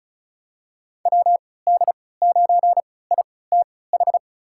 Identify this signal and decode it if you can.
Morse code, heard as a keyed tone, about 34 wpm, spaced as for 20 wpm, 700 Hz. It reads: WD9ITH
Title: USA Call Signs - 6 Characters - 3x Character Spacing - 35wpm